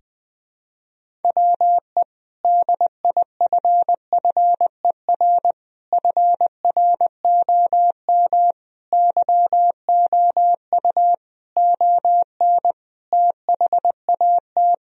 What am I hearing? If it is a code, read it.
Morse code, heard as a keyed tone, about 20 wpm, 700 Hz. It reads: WE DIFFER FROM YOU ON THAT
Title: Sentences from Top 1000 Words - Review Part 1 - 20wpm